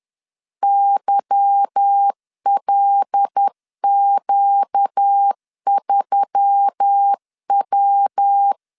Morse code, heard as a keyed tone, about 10 wpm, 800 Hz. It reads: YLQ3W
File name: LEZIONI / 8 YJ45 / 8 Y 10wpm 2615